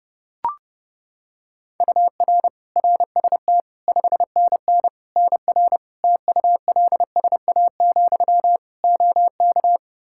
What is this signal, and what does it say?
Morse code, heard as a keyed tone, about 30 wpm, 700 Hz. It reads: UR RST 5NN NR TULSA, OK
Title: Practice QSO - Exchange 1 - Beginning - 30wpm